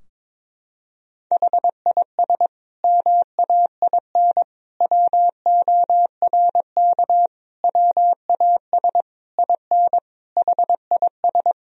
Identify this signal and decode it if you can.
Morse code, heard as a keyed tone, about 22 wpm, 700 Hz. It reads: HIS MAIN WORK WAS IN HIS
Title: Sentences from Top 1000 Words - Review Part 2 - 22wpm